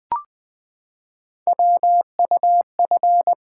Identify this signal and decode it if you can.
Morse code, heard as a keyed tone, about 20 wpm, 700 Hz. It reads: WUF
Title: Three Letter-Number Combinations - 20wpm